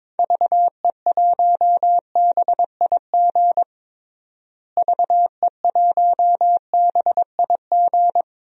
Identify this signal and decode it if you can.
Morse code, heard as a keyed tone, about 22 wpm, 700 Hz. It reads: VE1BIG VE1BIG
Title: DXCC-100 Call Signs - 6 Characters - Repeated 3x - 22wpm